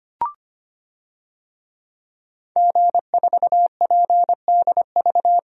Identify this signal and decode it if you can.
Morse code, heard as a keyed tone, about 25 wpm, 700 Hz. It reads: G4PDV
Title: DXCC-100 Call Signs - 5 Characters - 25wpm